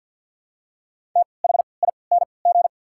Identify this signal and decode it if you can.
Morse code, heard as a keyed tone, about 53 wpm, spaced as for 29 wpm, 700 Hz. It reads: THING
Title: 3x Character Spacing - Top 200 Words - 50wpm